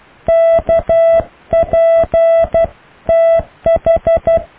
Morse code, heard as a keyed tone, about 12 wpm, 660 Hz. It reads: KP TH
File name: cw / EKTPV+BFHNW